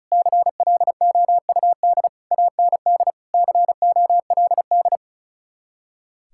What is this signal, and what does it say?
Morse code, heard as a keyed tone, about 43 wpm, 700 Hz. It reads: CLOUD AND COLD
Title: Speed-Racing Common QSO Phrases - 35wpm